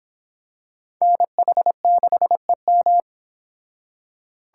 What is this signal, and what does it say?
Morse code, heard as a keyed tone, about 25 wpm, 700 Hz. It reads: NH6EM